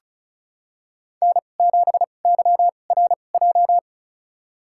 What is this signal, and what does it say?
Morse code, heard as a keyed tone, about 34 wpm, spaced as for 25 wpm, 700 Hz. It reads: N7YRJ
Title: USA Call Signs - 5 Characters - 2x Character Spacing - 35wpm